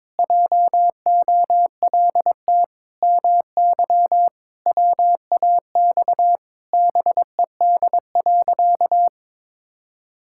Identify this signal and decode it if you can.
Morse code, heard as a keyed tone, about 22 wpm, 700 Hz. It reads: JOLT MY WAX BED.
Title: Pangram Challenge v2 - 22wpm